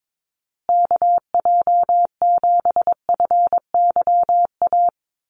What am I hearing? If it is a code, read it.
Morse code, heard as a keyed tone, about 22 wpm, 700 Hz. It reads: KJ7FYA
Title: USA Call Signs Repeated 2x 22wpm